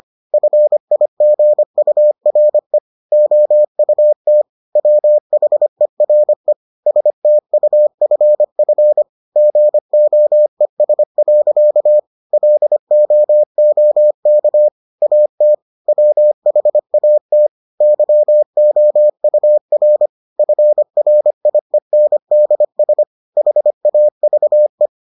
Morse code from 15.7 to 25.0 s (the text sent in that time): WHAT YOUR FRIENDS HAVE